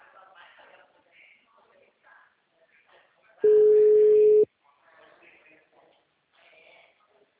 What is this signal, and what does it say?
Ringback tone (Europe, 425 Hz), with speech.